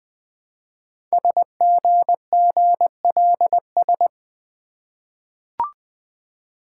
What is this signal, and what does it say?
Morse code, heard as a keyed tone, about 20 wpm, 700 Hz. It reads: SGGLS
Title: Code Groups - 5 Letters - 20wpm